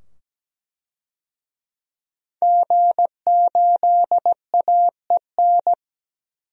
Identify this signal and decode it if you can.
Morse code, heard as a keyed tone, about 17 wpm, 700 Hz. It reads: G8AEN
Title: DXCC-100 Call Signs - 5 Characters - 17wpm